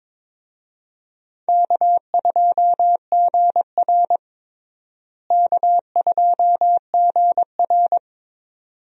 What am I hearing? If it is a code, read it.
Morse code, heard as a keyed tone, about 22 wpm, 700 Hz. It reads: K2GR K2GR
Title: USA Call Signs - 4 Characters - Repeated 2x - 22wpm